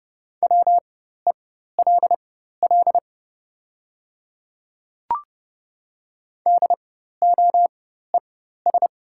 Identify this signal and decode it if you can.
Morse code, heard as a keyed tone, about 30 wpm, spaced as for 14 wpm, 700 Hz. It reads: WELL DOES
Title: 4x Character Spacing - Top 200 Words - 30wpm